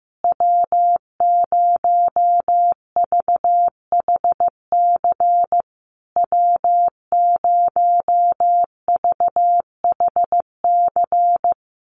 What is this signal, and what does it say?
Morse code, heard as a keyed tone, about 15 wpm, 700 Hz. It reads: W0VHC W0VHC
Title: USA Call Signs Repeated 3x 15wpm